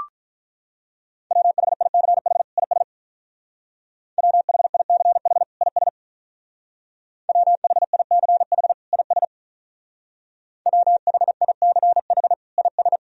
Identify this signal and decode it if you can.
Morse code, heard as a keyed tone, about 45 wpm, 700 Hz. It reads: WHICH IS WHICH IS WHICH IS WHICH IS